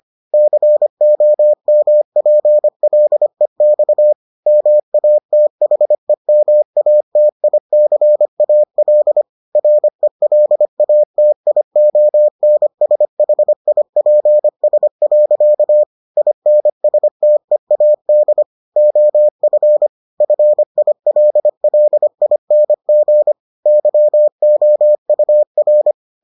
Morse code, heard as a keyed tone, about 25 wpm, 600 Hz. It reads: COMPLEX MATHEMATICAL RELATIONSHIPS. INSTEAD OF FILLING YOUR